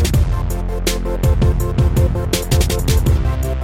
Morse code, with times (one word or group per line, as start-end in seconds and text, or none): none